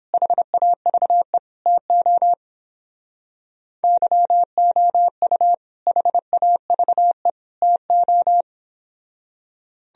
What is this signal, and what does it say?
Morse code, heard as a keyed tone, about 29 wpm, 700 Hz. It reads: HAVE TO YOU HAVE TO